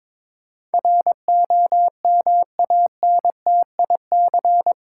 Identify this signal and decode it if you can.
Morse code, heard as a keyed tone, about 22 wpm, 700 Hz. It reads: ROMANTIC
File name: practice / 22wpm / Eight Letter Words - 22wpm